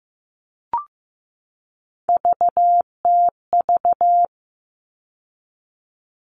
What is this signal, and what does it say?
Morse code, heard as a keyed tone, about 15 wpm, 700 Hz. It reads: VTV
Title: All Cut Numbers - 3-Digit 15wpm